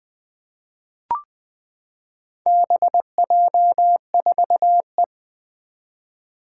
Morse code, heard as a keyed tone, about 21 wpm, 700 Hz. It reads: BJ4E
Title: Four Challenging Letter-Number Combinations 20wpm